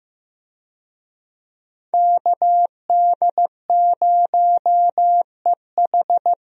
Morse code, heard as a keyed tone, about 15 wpm, 700 Hz. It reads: KD0EH
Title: USA Call Signs - 5 Characters - 15wpm